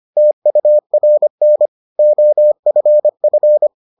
Morse code, heard as a keyed tone, about 25 wpm, 600 Hz. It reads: TURN OFF